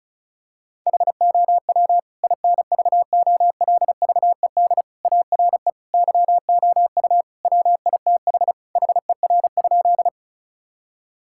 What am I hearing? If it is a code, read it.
Morse code, heard as a keyed tone, about 34 wpm, 690 Hz. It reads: HOW INVOLVED ARE YOU WITH HER?
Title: Sentences from Top 1000 Words - Review Part 5 - 35wpm